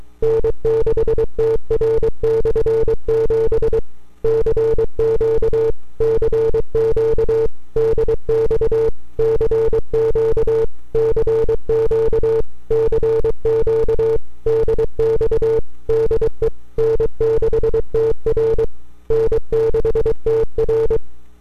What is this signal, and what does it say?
Morse code, heard as a keyed tone, about 23 wpm, 450 Hz. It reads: N6TR/7 CQ CQ DX CQ CQ CQ DX DE N6TR N6TR
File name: n6tr / 12Oct2009-160